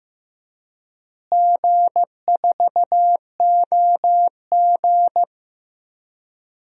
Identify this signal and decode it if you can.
Morse code, heard as a keyed tone, about 15 wpm, 700 Hz. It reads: G4OG
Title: DXCC-100 Call Signs - 4 Characters - Repeated 3x - 15wpm